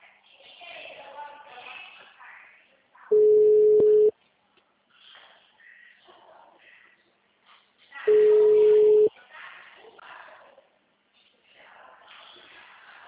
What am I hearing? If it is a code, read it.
Ringback tone (Europe, 425 Hz), with speech.
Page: Itu 2